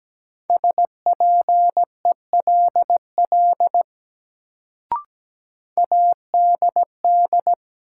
Morse code, heard as a keyed tone, about 17 wpm, 700 Hz. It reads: SPELL ADD
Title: Sentences from Top 1000 Words - Review Part 1 - New Word Drill - 17wpm